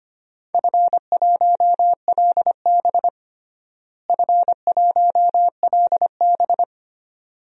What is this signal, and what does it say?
Morse code, heard as a keyed tone, about 25 wpm, 700 Hz. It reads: F1LB F1LB
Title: DXCC-100 Call Signs - 4 Characters - Repeated 2x - 25wpm